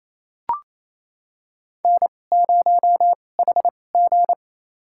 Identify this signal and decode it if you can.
Morse code, heard as a keyed tone, about 28 wpm, spaced as for 20 wpm, 700 Hz. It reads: N0HG
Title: USA Call Signs - 4 Characters - 2x Character Spacing - 28wpm